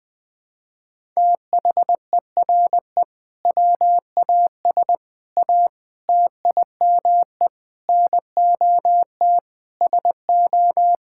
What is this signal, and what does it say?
Morse code, heard as a keyed tone, about 20 wpm, 700 Hz. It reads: THERE WAS A TIME NOT SO
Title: Sentences with 8-Word Phrases - 20wpm